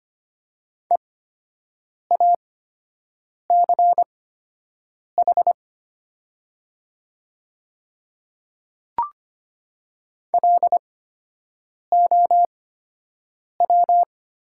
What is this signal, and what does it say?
Morse code, heard as a keyed tone, about 25 wpm, spaced as for 7 wpm, 700 Hz. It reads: EACH LOW
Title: 8x Character Spacing - Top 200 Words - 25wpm